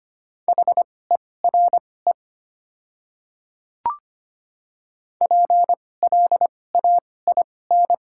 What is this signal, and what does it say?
Morse code, heard as a keyed tone, about 25 wpm, spaced as for 18 wpm, 700 Hz. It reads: HERE PLAIN
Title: 2x Character Spacing - Top 500 Words - 25wpm